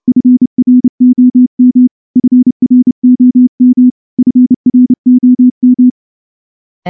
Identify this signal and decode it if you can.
Morse code, heard as a keyed tone, about 28 wpm, 250 Hz. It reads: FROM FROM FROM
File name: morse-training-30WPM